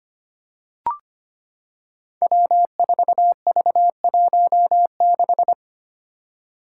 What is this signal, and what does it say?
Morse code, heard as a keyed tone, about 25 wpm, 700 Hz. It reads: W4V16
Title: Five Challenging Letter-Number Combinations 25wpm